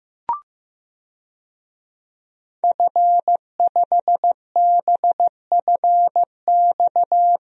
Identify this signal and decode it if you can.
Morse code, heard as a keyed tone, about 15 wpm, 700 Hz. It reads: F5BFX
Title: DXCC-100 Call Signs - 5 Characters - Repeated 3x - 15wpm